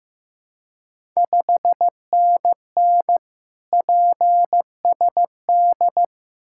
Morse code, heard as a keyed tone, about 15 wpm, 700 Hz. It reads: 5NN PSD
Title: Ontario QSO Party Exchange 15wpm